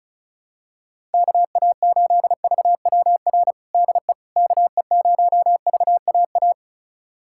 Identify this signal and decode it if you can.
Morse code, heard as a keyed tone, about 36 wpm, 700 Hz. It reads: KA8VWR DE KE0VAA
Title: USA Callsign DE Callsign - 6 Characters - 35wpm